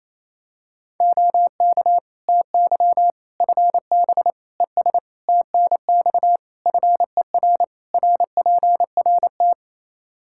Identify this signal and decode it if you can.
Morse code, heard as a keyed tone, about 28 wpm, 700 Hz. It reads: OK TY FB ES TNX FER RPRT
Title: Practice QSO - Exchange 2 - Info - 28wpm